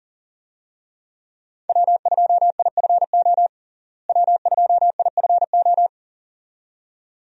Morse code, heard as a keyed tone, about 40 wpm, 700 Hz. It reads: W2IFO W2IFO
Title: USA Call Signs - 5 Characters - Repeated 2x - 40wpm